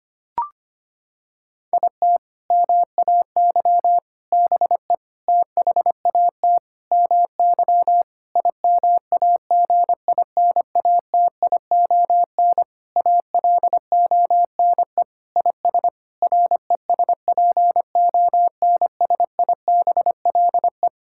Morse code, heard as a keyed tone, about 25 wpm, 700 Hz. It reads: IT MAY BE THAT MY IMAGINATION ALONE IS RESPONSIBLE
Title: Sentences from Top 1600 Words 25wpm